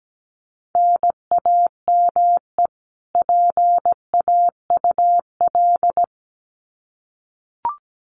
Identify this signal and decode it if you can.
Morse code, heard as a keyed tone, about 17 wpm, 700 Hz. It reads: NAME PAUL
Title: Common QSO Phrases 17wpm